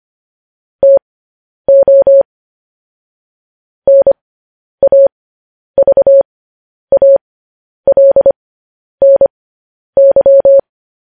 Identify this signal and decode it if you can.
Morse code, heard as a keyed tone, about 25 wpm, spaced as for 10 wpm, 550 Hz. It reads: TO NAVALNY